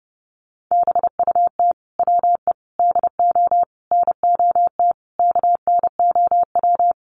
Morse code, heard as a keyed tone, about 30 wpm, 700 Hz. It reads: BUT WE DO NOT KNOW
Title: Sets of 5 Words Encore2 30wpm